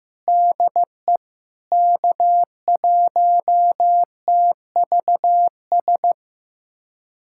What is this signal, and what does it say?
Morse code, heard as a keyed tone, about 16 wpm, 700 Hz. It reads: DE K1TVS
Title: USA Callsign DE Callsign - 5 Characters - 15wpm